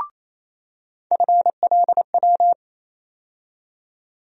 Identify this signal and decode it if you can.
Morse code, heard as a keyed tone, about 28 wpm, 700 Hz. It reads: FLW
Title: USA Call Sign Postfixes - 3 Letters 28wpm